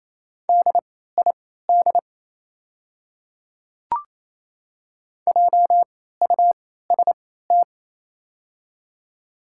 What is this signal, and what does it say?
Morse code, heard as a keyed tone, about 28 wpm, spaced as for 16 wpm, 690 Hz. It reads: DID JUST
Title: 3x Character Spacing - Top 200 Words - 28wpm